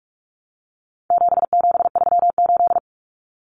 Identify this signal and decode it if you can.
Morse code, heard as a keyed tone, about 44 wpm, 700 Hz. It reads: 7738
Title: Numbers - 4-Digit 45wpm